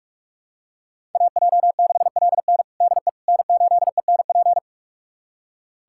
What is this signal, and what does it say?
Morse code, heard as a keyed tone, about 42 wpm, 700 Hz. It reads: AJ6LN DE N8ENP
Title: USA Callsign DE Callsign - 5 Characters - 45wpm